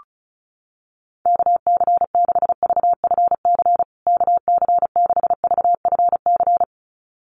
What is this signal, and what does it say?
Morse code, heard as a keyed tone, about 34 wpm, 700 Hz. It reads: KC6VFC KC6VFC
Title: USA Call Signs Repeated 2x 35wpm